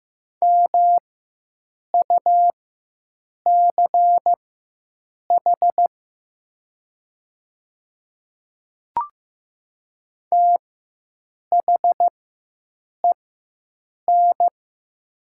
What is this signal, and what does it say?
Morse code, heard as a keyed tone, about 15 wpm, spaced as for 7 wpm, 700 Hz. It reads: MUCH THEN
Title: 4x Character Spacing - Top 200 Words - 15wpm